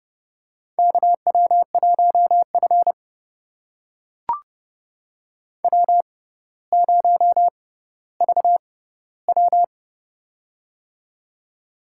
Morse code, heard as a keyed tone, about 30 wpm, spaced as for 15 wpm, 700 Hz. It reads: KW1F W0VW
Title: USA Call Signs - 4 Characters - 6x Character Spacing - 30wpm